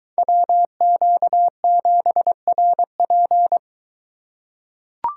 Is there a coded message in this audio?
Yes: Morse code from 0.2 to 3.6 s, heard as a keyed tone, about 22 wpm, 700 Hz. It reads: WQ7RP